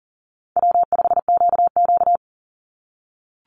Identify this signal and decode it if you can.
Morse code, heard as a keyed tone, about 40 wpm, 700 Hz. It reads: W5QQ
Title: Easy Call Signs - 40wpm